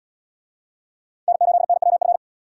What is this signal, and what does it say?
Morse code, heard as a keyed tone, about 68 wpm, 700 Hz. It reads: N8NWF